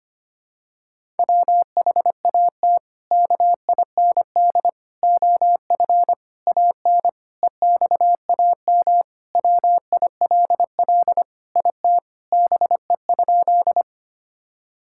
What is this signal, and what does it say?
Morse code, heard as a keyed tone, about 25 wpm, 700 Hz. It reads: WHAT KIND OF AN EXAM WILL IT BE?
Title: Sentences from Top 1000 Words - Review Part 5 - 25wpm